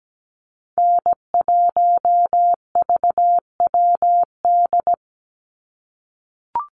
Morse code, heard as a keyed tone, about 17 wpm, 700 Hz. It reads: N1VWD
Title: NA SOTA Chaser Callsigns 17wpm